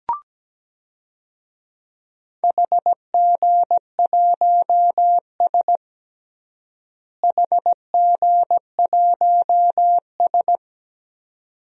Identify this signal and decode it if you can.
Morse code, heard as a keyed tone, about 17 wpm, 700 Hz. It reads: HG1S HG1S
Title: DXCC-100 Call Signs - 4 Characters - Repeated 2x - 17wpm